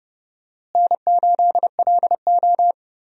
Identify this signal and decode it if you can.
Morse code, heard as a keyed tone, about 30 wpm, 700 Hz. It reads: N8LO
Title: USA Call Signs - 4 Characters - Repeated 2x - 30wpm